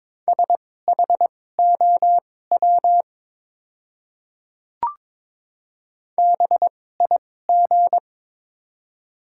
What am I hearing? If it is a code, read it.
Morse code, heard as a keyed tone, about 22 wpm, spaced as for 16 wpm, 700 Hz. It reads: SHOW BIG
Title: 2x Character Spacing - Top 300 Words - 22wpm